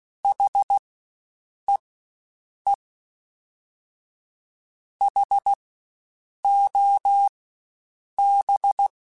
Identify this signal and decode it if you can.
Morse code, heard as a keyed tone, about 16 wpm, spaced as for 7 wpm, 800 Hz. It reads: HEE HOB
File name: Spaced Alphas / Spaced Alphas mp3s / Hs sp @15